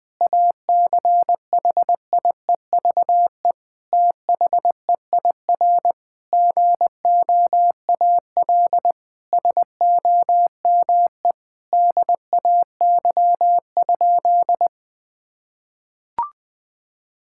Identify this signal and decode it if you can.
Morse code, heard as a keyed tone, about 20 wpm, 700 Hz. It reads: ACHIEVE THEIR GOAL SOME DAY?
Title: Sentences from Top 1000 Words - Review Part 6 - 20wpm